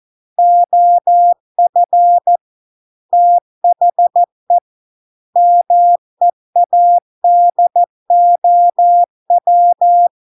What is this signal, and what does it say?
Morse code, heard as a keyed tone, about 14 wpm, 700 Hz. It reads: OF THE MEADOW